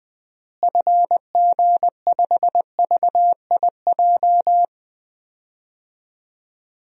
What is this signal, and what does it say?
Morse code, heard as a keyed tone, about 20 wpm, 700 Hz. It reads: FG5VIJ
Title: DXCC-100 Call Signs - 6 Characters - Repeated 2x - 20wpm